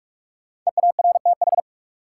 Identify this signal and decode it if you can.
Morse code, heard as a keyed tone, about 46 wpm, 700 Hz. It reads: EARTH
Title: Sentences from Top 200 Words - New Word Drill - 45wpm